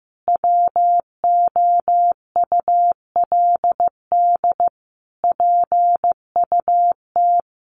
Morse code, heard as a keyed tone, about 15 wpm, 700 Hz. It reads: WOULD PUT
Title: Easy Phrases 15wpm